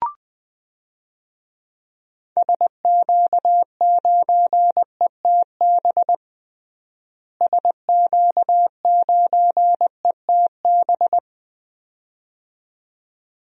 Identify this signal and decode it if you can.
Morse code, heard as a keyed tone, about 20 wpm, 700 Hz. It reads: SQ9ETB SQ9ETB
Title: DXCC-100 Call Signs - 6 Characters - Repeated 2x - 20wpm